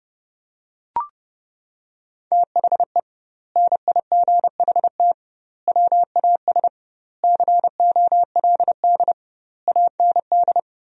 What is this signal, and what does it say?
Morse code, heard as a keyed tone, about 32 wpm, 700 Hz. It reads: THE NIGHT WAS COLD AND
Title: 2x Word-Spacing - Sentences from Top 400 Words - 30wpm